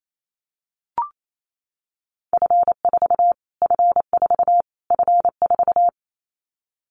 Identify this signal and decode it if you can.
Morse code, heard as a keyed tone, about 28 wpm, 700 Hz. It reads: F4 F4 F4
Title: Callsign Prefixes for Top 25 DXCC - Repeated 3x - 28wpm